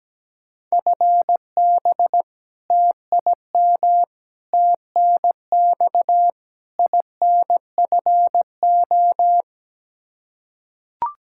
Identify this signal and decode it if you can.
Morse code, heard as a keyed tone, about 17 wpm, 700 Hz. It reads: FB TIM TNX INFO
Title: Practice QSO - Exchange 3 - More Info - 17wpm